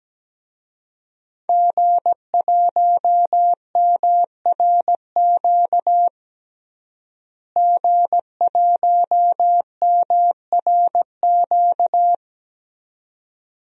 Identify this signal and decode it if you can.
Morse code, heard as a keyed tone, about 17 wpm, 700 Hz. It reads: G1MRQ G1MRQ
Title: DXCC-100 Call Signs - 5 Characters - Repeated 2x - 17wpm